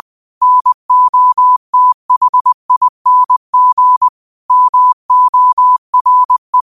Morse code, heard as a keyed tone, about 20 wpm, 1000 Hz. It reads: NOTHING MORE